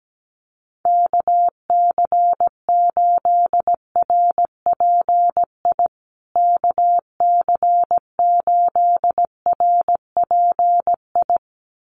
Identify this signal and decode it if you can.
Morse code, heard as a keyed tone, about 17 wpm, 700 Hz. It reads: KC8RPI KC8RPI
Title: USA Call Signs Repeated 2x 17wpm